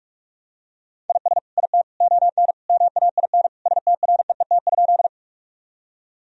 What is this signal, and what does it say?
Morse code, heard as a keyed tone, about 42 wpm, 700 Hz. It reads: IS IT ON MAIN STREET?
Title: Sentences from Top 1000 Words - Review Part 2 - 45wpm